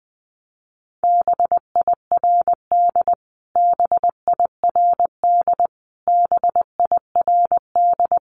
Morse code, heard as a keyed tone, about 20 wpm, 700 Hz. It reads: BIRD BIRD BIRD
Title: Top 500 Words Repeated 20wpm